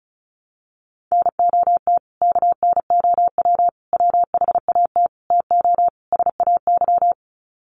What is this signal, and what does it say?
Morse code, heard as a keyed tone, about 34 wpm, 700 Hz. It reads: NOT KNOW WHAT TO SAY
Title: Sets of 5 Words 35wpm